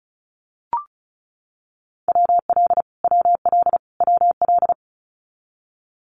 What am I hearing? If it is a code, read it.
Morse code, heard as a keyed tone, about 34 wpm, 700 Hz. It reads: WL WL WL
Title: Abbreviations Repeated 3x - 35wpm